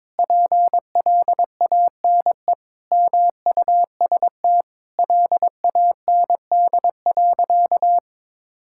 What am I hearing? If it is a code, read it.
Morse code, heard as a keyed tone, about 22 wpm, 700 Hz. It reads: PLANE MUST LAND.
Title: Sentences from Top 1000 Words - Review Part 6 - 22wpm